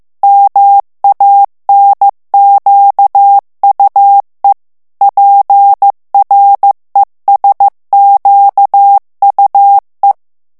Morse code, heard as a keyed tone, about 15 wpm, 800 Hz. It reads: MANQUE PRESQUE
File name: morse mp3 / FRANCAIS TEXTE CLAIR 1 VITESSE 14 MOTS MINUTES